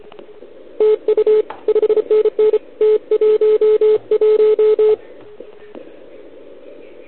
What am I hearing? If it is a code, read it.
Morse code, heard as a keyed tone, about 29 wpm, 410 Hz. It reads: TU 5NN T11
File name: ARCHIV / OKDX2020 / OL4Wa